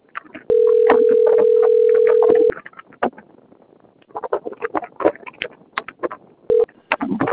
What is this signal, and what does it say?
Ringback tone (North America), 440 + 480 Hz.